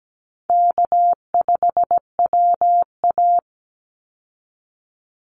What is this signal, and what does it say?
Morse code, heard as a keyed tone, about 17 wpm, 700 Hz. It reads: K5WA
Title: Easy Call Signs Repeated 3x - 17wpm